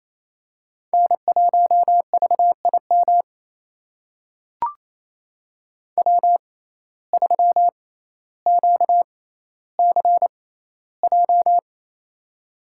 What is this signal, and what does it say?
Morse code, heard as a keyed tone, about 28 wpm, spaced as for 14 wpm, 700 Hz. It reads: N1VIM W3QCJ
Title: USA Call Signs - 5 Characters - 6x Character Spacing - 28wpm